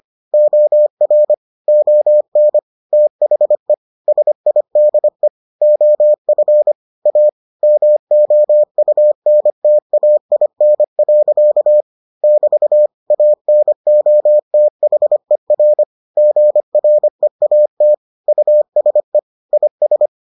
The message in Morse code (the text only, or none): OR ON THE SIDE OF A MOUNTAIN. <BT> ANOTHER GREAT USE IS